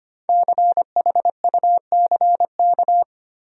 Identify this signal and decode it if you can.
Morse code, heard as a keyed tone, about 25 wpm, 700 Hz. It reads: CHUCK
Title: USA Common Names - 25wpm